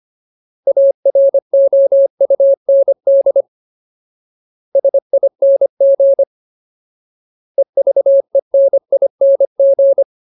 Morse code, heard as a keyed tone, about 25 wpm, 550 Hz. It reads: AROUND SING EVENING
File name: Ultimate Word List / 25WPM Ultimate